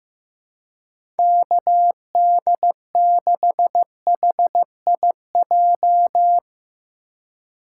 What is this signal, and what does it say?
Morse code, heard as a keyed tone, about 15 wpm, 700 Hz. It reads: KD6HIJ
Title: USA Call Signs - 6 Characters - 4x Character Spacing - 15wpm